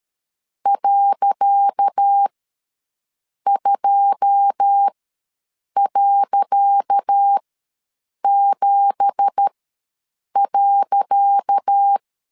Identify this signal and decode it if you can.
Morse code, heard as a keyed tone, about 13 wpm, spaced as for 7 wpm, 800 Hz. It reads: .2.7.
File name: LEZIONI / 10 SIGNS / 10C DOT ws 12wpm 3548